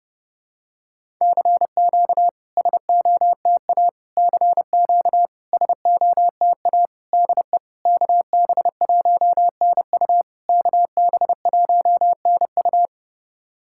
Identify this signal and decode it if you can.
Morse code, heard as a keyed tone, about 30 wpm, 690 Hz. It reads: CQ SOTA CQ SOTA DE KB1NU KB1NU
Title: SOTA QSO Practice 30wpm